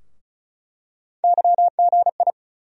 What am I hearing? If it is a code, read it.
Morse code, heard as a keyed tone, about 36 wpm, 700 Hz. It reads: YGI
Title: USA Call Sign Postfixes - 3 Letters 35wpm